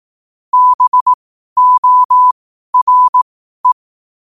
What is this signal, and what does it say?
Morse code, heard as a keyed tone, about 18 wpm, spaced as for 12 wpm, 1000 Hz. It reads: BORE
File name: qotdfiles / 12wpm